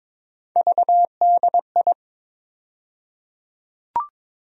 Morse code, heard as a keyed tone, about 22 wpm, 700 Hz. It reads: VDI
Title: USA Call Sign Postfixes - 3 Letters 22wpm